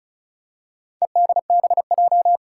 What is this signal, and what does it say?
Morse code, heard as a keyed tone, about 36 wpm, 700 Hz. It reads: EDBJ
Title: Four Challenging Letter-Number Combinations 35wpm